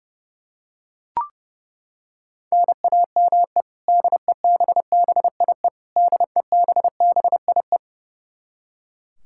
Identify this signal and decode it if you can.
Morse code, heard as a keyed tone, about 32 wpm, 700 Hz. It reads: NAME DEBBIE DEBBIE
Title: Practice QSO - Exchange 1 - Beginning - 30wpm